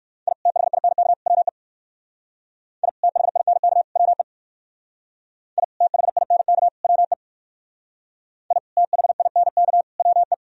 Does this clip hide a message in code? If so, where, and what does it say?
Yes: Morse code from 0.3 to 10.3 s, heard as a keyed tone, about 59 wpm, 700 Hz. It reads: I THINK WE I THINK WE I THINK WE I THINK WE